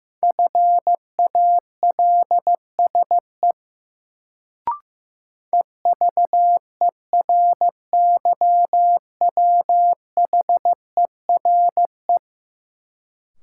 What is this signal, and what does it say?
Morse code, heard as a keyed tone, about 15 wpm, 700 Hz. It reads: FALSE EVERYWHERE
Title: Sentences from Top 1600 Words - New Word Drill Alt - 15wpm